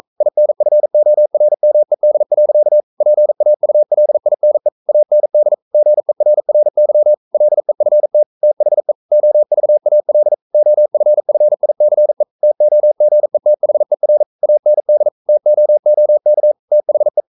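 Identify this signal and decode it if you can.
Morse code, heard as a keyed tone, about 42 wpm, 600 Hz. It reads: INFORMED. PAULINE AND GERRY LEFT THE OVAL OFFICE TOGETHER AND TOOK THE
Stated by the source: Morse code sending of an English novel at 42 wpm.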